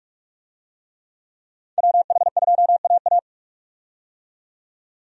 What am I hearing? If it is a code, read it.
Morse code, heard as a keyed tone, about 44 wpm, 700 Hz. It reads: WH2AA
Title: USA Call Signs - 5 Characters - 45wpm